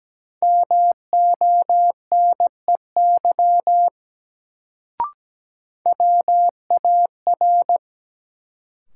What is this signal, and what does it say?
Morse code, heard as a keyed tone, about 17 wpm, 700 Hz. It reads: MONEY WAR
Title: Sentences from Top 400 Words - New Word Drill - 17wpm